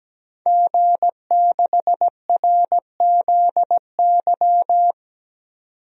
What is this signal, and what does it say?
Morse code, heard as a keyed tone, about 17 wpm, 700 Hz. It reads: G6RZY
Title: DXCC-100 Call Signs - 5 Characters - 17wpm